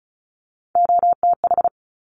Morse code, heard as a keyed tone, about 36 wpm, 700 Hz. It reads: OTH
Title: Top Three Letter Combinations in Words 35wpm